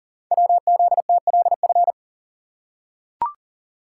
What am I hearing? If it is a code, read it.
Morse code, heard as a keyed tone, about 41 wpm, 700 Hz. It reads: WZTLF
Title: Code Groups - 5 Letters - 40wpm